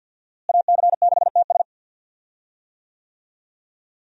Morse code, heard as a keyed tone, about 52 wpm, 700 Hz. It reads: AC6TS
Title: USA Call Signs - 5 Characters - Repeated 2x - 50wpm